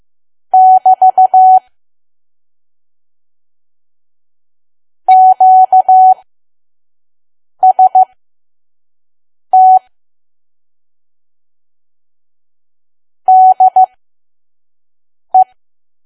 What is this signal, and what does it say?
Morse code, heard as a keyed tone, about 15 wpm, spaced as for 5 wpm, 750 Hz. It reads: <BT> QST DE
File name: Archive / 5 WPM files / PM